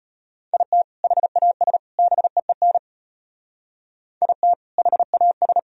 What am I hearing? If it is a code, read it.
Morse code, heard as a keyed tone, about 32 wpm, 700 Hz. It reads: IT HAS BEEN IT HAS